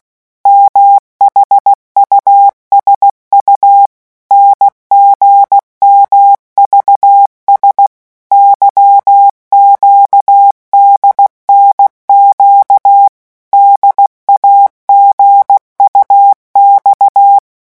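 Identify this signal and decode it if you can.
Morse code, heard as a keyed tone, about 16 wpm, 800 Hz. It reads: MHUSU NGMVS YQDNQ DAGUX
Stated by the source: Morse letter-group revision drill at 15 words per minute.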